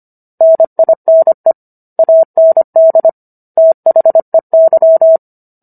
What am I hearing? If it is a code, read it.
Morse code, heard as a keyed tone, about 25 wpm, 650 Hz. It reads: NINE AND THEY
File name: Book One / 16-Chapter 16 - 25 WPM